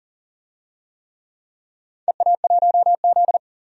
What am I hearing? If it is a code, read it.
Morse code, heard as a keyed tone, about 40 wpm, 700 Hz. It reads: EA1Z
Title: DXCC-100 Call Signs - 4 Characters - 40wpm